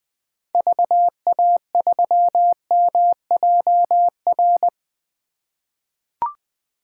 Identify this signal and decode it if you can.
Morse code, heard as a keyed tone, about 21 wpm, 700 Hz. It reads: VA3MJR
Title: 2023 POTA Hunter Call Signs - 20wpm